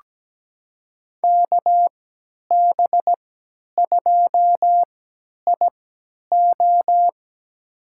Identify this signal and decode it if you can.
Morse code, heard as a keyed tone, about 17 wpm, spaced as for 10 wpm, 700 Hz. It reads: KB2IO
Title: USA Call Signs - 5 Characters - 3x Character Spacing - 17wpm